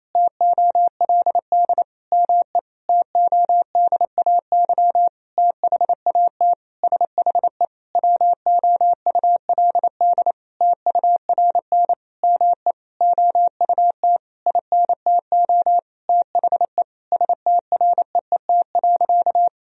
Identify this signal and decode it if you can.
Morse code, heard as a keyed tone, about 28 wpm, 700 Hz. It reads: TOLD ME TODAY THAT SHE WOULD TURN ME OUT INTO THE STREET.
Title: Sentences from Top 1000 Words - Review Part 4 - 28wpm